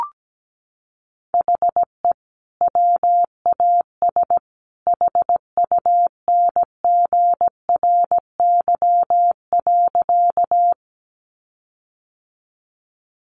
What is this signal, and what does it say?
Morse code, heard as a keyed tone, about 17 wpm, 700 Hz. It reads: HE WAS HUNGRY.
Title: Sentences from Top 1400 Words 17wpm